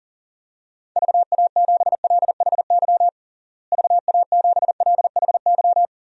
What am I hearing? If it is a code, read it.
Morse code, heard as a keyed tone, about 40 wpm, 700 Hz. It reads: VA7LHY VA7LHY
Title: DXCC-100 Call Signs - 6 Characters - Repeated 3x - 40wpm